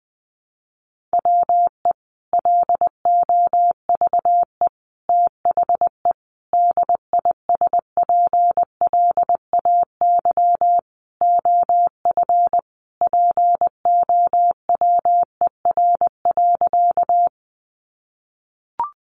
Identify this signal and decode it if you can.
Morse code, heard as a keyed tone, about 20 wpm, 700 Hz. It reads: WE LOVE THE DISPLAY OF POWER.
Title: Sentences from Top 1200 Words 20wpm